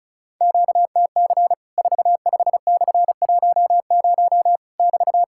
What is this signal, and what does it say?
Morse code, heard as a keyed tone, about 35 wpm, 700 Hz. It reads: QTC 45/10 <BT>
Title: Worked All Europe DX Contest - QTC Exchange - 35wpm